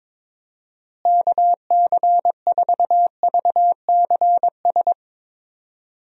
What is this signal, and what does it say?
Morse code, heard as a keyed tone, about 22 wpm, 700 Hz. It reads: KC4VCS
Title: USA Call Signs - 6 Characters - Repeated 2x - 22wpm